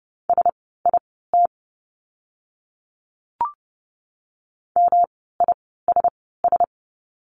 Morse code, heard as a keyed tone, about 30 wpm, spaced as for 17 wpm, 700 Hz. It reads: SIT MISS
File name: course / 30:10wpm / 108 Lesson 18f - Letter m - Words 30wpm